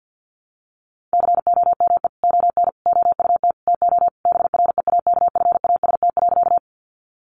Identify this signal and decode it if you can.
Morse code, heard as a keyed tone, about 50 wpm, 700 Hz. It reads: COME ON OUT TO BREAKFAST.
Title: Sentences from Top 1500 Words 50wpm